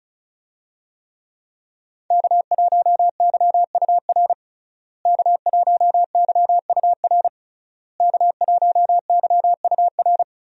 Morse code, heard as a keyed tone, about 36 wpm, 700 Hz. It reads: K1YUR K1YUR K1YUR
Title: USA Call Signs - 5 Characters - Repeated 3x - 35wpm